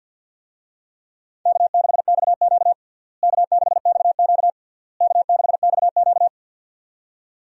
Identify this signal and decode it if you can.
Morse code, heard as a keyed tone, about 52 wpm, 700 Hz. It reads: K6XQ K6XQ K6XQ
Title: USA Call Signs - 4 Characters - Repeated 3x - 50wpm